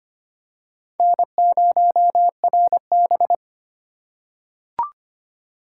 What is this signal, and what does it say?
Morse code, heard as a keyed tone, about 25 wpm, 700 Hz. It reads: N0RB
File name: practice / 25wpm / Four Letter-Number Combinations - 25wpm